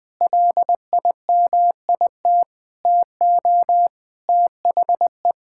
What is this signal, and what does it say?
Morse code, heard as a keyed tone, about 20 wpm, 700 Hz. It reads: LIMIT TO THE
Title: Sets of 8 Words - 20wpm